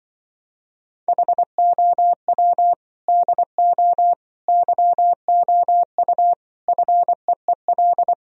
Morse code, heard as a keyed tone, about 24 wpm, 700 Hz. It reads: HOW DO YOU FEEL
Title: Speed-Racing - Sets of 5 Words 20wpm